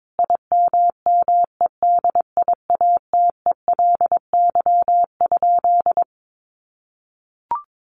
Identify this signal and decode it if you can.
Morse code, heard as a keyed tone, about 22 wpm, 700 Hz. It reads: IMMEDIATELY?
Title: Sentences from Top 1300 Words 22wpm